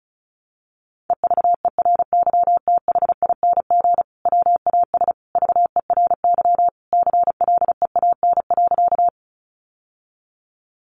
Morse code, heard as a keyed tone, about 34 wpm, 700 Hz. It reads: EVERYTHING WAS VERY CLEAN.
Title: Sentences from Top 1100 Words 35wpm